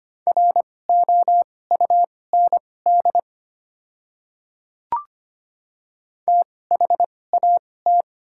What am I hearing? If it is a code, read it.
Morse code, heard as a keyed tone, about 25 wpm, spaced as for 18 wpm, 700 Hz. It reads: ROUND THAT
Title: 2x Character Spacing - Top 200 Words - 25wpm